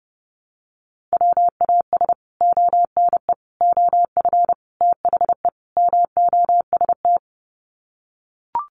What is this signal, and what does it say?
Morse code, heard as a keyed tone, about 30 wpm, 700 Hz. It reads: WAS ONE OF THE MOST
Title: Sets of 5 Words 30wpm